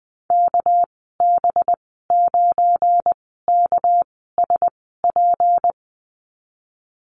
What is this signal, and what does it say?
Morse code, heard as a keyed tone, about 20 wpm, spaced as for 15 wpm, 700 Hz. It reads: KB9KSP
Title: 238 Lesson 38g - Letter q - Call Signs 20wpm